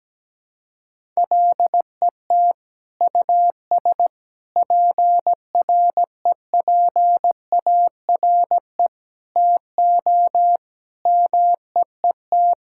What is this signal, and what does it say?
Morse code, heard as a keyed tone, about 17 wpm, 700 Hz. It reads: LET US PREPARE TO MEET
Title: Sentences from Top 1000 Words - Review Part 6 - 17wpm